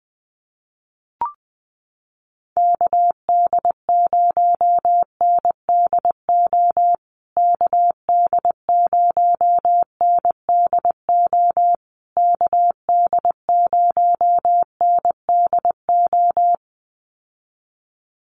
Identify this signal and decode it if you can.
Morse code, heard as a keyed tone, about 20 wpm, 700 Hz. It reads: KD0NDO KD0NDO KD0NDO
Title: USA Call Signs Repeated 3x 20wpm